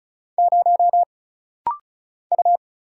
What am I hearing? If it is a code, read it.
Morse code, heard as a keyed tone, about 36 wpm, 700 Hz. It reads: 0 U
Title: Single Letter-Number - Rapid-Fire 35wpm